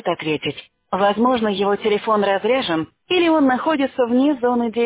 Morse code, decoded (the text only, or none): none